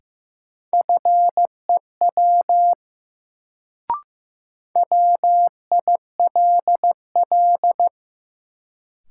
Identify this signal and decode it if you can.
Morse code, heard as a keyed tone, about 15 wpm, 700 Hz. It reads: FEW WILL
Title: Sets of 2 Words - New Word Drill - 15wpm